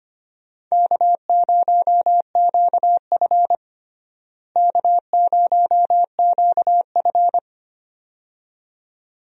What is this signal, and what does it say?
Morse code, heard as a keyed tone, about 25 wpm, 700 Hz. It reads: K0QF K0QF
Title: USA Call Signs - 4 Characters - Repeated 2x - 25wpm